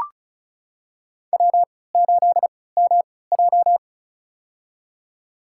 Morse code, heard as a keyed tone, about 34 wpm, spaced as for 20 wpm, 700 Hz. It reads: W8MJ
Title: USA Call Signs - 3x Character Spacing - 35wpm